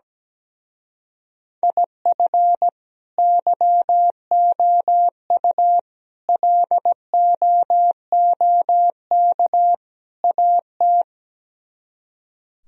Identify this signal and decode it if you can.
Morse code, heard as a keyed tone, about 17 wpm, 700 Hz. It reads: IF YOU LOOK AT
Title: Speed-Racing - Sets of 4 Words 17wpm